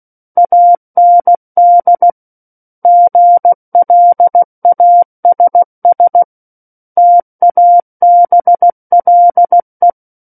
Morse code, heard as a keyed tone, about 16 wpm, 700 Hz. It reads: AND GLASS TABLE
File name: Book One / 07-Chapter 7 - 16 WPM